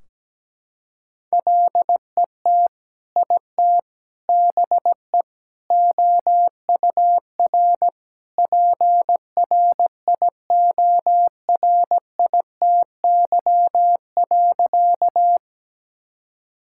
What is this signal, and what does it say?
Morse code, heard as a keyed tone, about 17 wpm, 700 Hz. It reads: LET IT BE OUR PRIORITY.
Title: Sentences from Top 1600 Words 17wpm